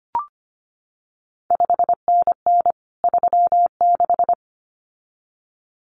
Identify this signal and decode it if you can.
Morse code, heard as a keyed tone, about 25 wpm, 700 Hz. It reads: 5NN 36
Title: CQ WW DX Contest Exchange Encore - 25wpm